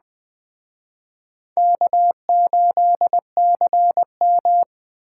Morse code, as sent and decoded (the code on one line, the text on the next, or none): -.- ---.. -.-. --
K8CM